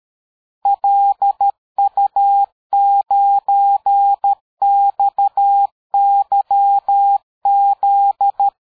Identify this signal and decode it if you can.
Morse code, heard as a keyed tone, about 13 wpm, 800 Hz. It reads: LU9XYZ
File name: dk5ke / QSO 60BpM